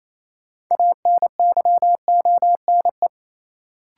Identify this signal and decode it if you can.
Morse code, heard as a keyed tone, about 28 wpm, 700 Hz. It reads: ANYONE
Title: Sets of 5 Words Encore - New Word Drill - 28wpm